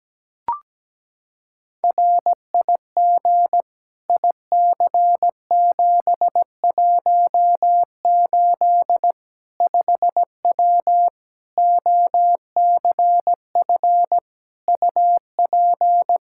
Morse code, heard as a keyed tone, about 17 wpm, 700 Hz. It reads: RIG IC718 5W OCF UP
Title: K1USN SST QSO Practice 17wpm